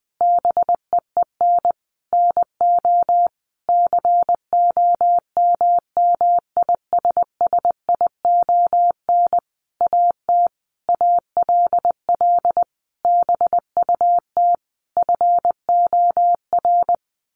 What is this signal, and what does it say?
Morse code, heard as a keyed tone, about 20 wpm, 700 Hz. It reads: BEEN NO COMMISSION AT ALL BUT FOR
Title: Sentences from Top 1400 Words 20wpm